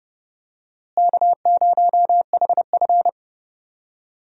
Morse code, heard as a keyed tone, about 30 wpm, 700 Hz. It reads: K0HF
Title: USA Call Signs - 4 Characters - Repeated 2x - 30wpm